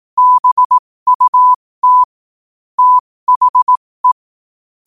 Morse code, heard as a keyed tone, about 18 wpm, spaced as for 15 wpm, 1000 Hz. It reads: BUT THE
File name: qotdfiles / 15wpm